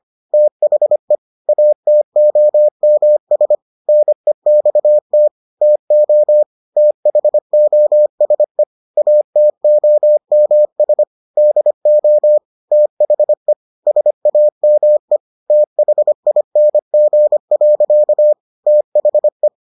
Morse code, heard as a keyed tone, about 25 wpm, 600 Hz. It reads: THE ATOMS NEXT TO THOSE ATOMS DO THE SAME THING. THE